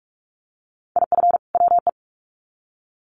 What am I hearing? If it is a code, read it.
Morse code, heard as a keyed tone, about 44 wpm, 700 Hz. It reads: IF WE
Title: Sets of 2 Words 45wpm